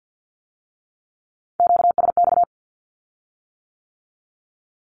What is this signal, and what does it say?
Morse code, heard as a keyed tone, about 50 wpm, 700 Hz. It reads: QSX
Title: Q-Codes with Spoken Meaning - 50wpm